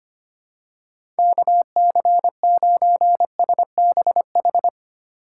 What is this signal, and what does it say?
Morse code, heard as a keyed tone, about 25 wpm, 700 Hz. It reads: KC9SBH
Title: USA Call Signs - 6 Characters - 8x Character Spacing - 25wpm